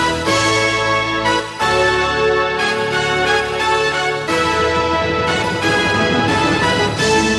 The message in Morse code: none